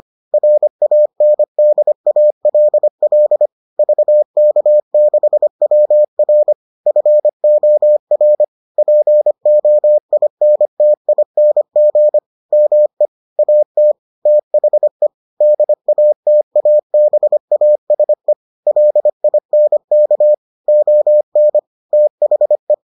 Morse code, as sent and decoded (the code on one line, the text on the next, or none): .-. .- -. -.. .- .-.. .-.. / ...- -.- -.... .-- .-. / ..-. --- .-. / .--. --- .. -. - .. -. --. / -- . / .- - / - .... / -.. .- - .- -... .- ... / .-.. .. -. -.- / --- -. / - .... .
RANDALL VK6WR FOR POINTING ME AT THE DATABASE LINK ON THE